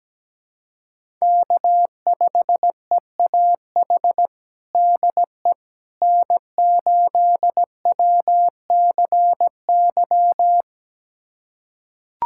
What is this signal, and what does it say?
Morse code, heard as a keyed tone, about 17 wpm, 700 Hz. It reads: K5EAH DE N8WCY
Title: USA Callsign DE Callsign - 5 Characters - 17wpm